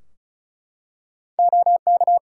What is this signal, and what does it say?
Morse code, heard as a keyed tone, about 36 wpm, 700 Hz. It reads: OK
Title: Practice QSO - Exchange 3 - More Info - 35wpm